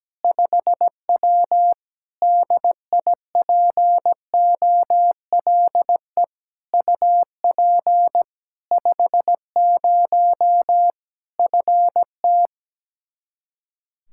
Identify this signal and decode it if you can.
Morse code, heard as a keyed tone, about 17 wpm, 700 Hz. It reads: 5W DIPOLE UP 50 FT
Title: Practice QSO - Exchange 2 - Info - 17wpm